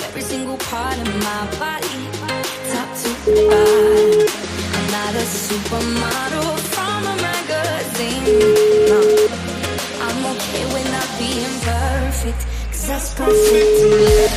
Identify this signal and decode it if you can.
Ringback tone (Europe, 425 Hz), with speech.